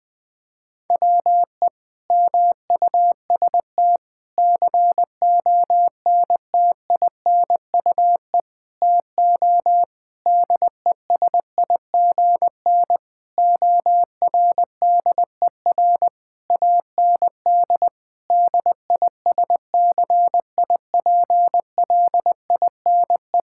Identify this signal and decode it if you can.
Morse code, heard as a keyed tone, about 20 wpm, 700 Hz. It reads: WE MUST CONTINUE TO DESIGN ORDER AND DISCIPLINE
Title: Sentences from Top 1000 Words - Review Part 6 - 20wpm